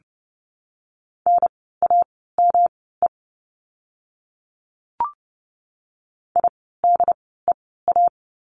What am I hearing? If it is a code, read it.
Morse code, heard as a keyed tone, about 31 wpm, spaced as for 17 wpm, 700 Hz. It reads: NAME IDEA
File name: course / 30:10wpm / 108 Lesson 18f - Letter m - Words 30wpm